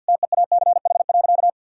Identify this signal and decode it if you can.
Morse code, heard as a keyed tone, about 50 wpm, 700 Hz. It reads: TEACH.
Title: Sentences from Top 1000 Words - Review Part 2 - 50wpm